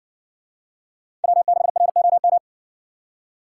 Morse code, heard as a keyed tone, about 60 wpm, 700 Hz. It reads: W6ROM